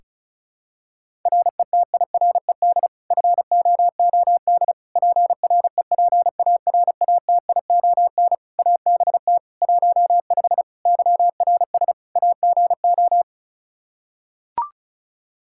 Morse code, heard as a keyed tone, about 34 wpm, 700 Hz. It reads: RETIRED FOOD PREPARATION ABT 15 YRS AGO
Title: K1USN SST QSO Practice 35wpm